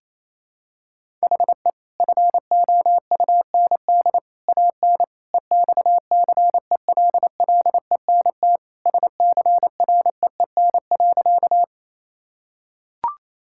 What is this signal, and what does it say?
Morse code, heard as a keyed tone, about 28 wpm, 700 Hz. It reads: HE FOUND AN EXCELLENT SCREEN.
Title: Sentences from Top 1000 Words - Review Part 6 - 28wpm